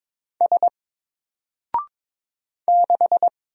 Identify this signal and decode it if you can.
Morse code, heard as a keyed tone, about 22 wpm, 700 Hz. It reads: S 6
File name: Single-Letter-Number-22wpm